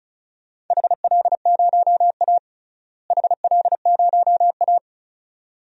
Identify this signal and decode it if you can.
Morse code, heard as a keyed tone, about 36 wpm, 700 Hz. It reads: HL0A HL0A
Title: DXCC-100 Call Signs - 4 Characters - Repeated 2x - 35wpm